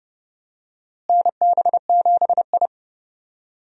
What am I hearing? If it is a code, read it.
Morse code, heard as a keyed tone, about 31 wpm, 700 Hz. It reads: NB7I
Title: USA Call Signs - 4 Characters - Repeated 3x - 30wpm